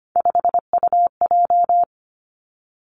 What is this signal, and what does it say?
Morse code, heard as a keyed tone, about 25 wpm, 700 Hz. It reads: 5UJ
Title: Three Challenging Letter-Number Combinations 25wpm